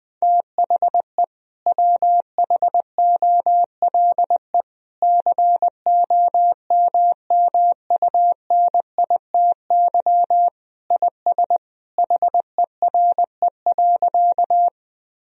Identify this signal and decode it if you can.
Morse code, heard as a keyed tone, about 20 wpm, 700 Hz. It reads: THE WHOLE COMMUNITY IS HERE.
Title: Sentences from Top 1000 Words - Review Part 4 - 20wpm